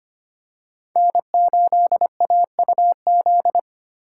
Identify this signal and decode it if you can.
Morse code, heard as a keyed tone, about 25 wpm, 700 Hz. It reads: N8AUZ
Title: USA Call Signs - 5 Characters - 4x Character Spacing - 25wpm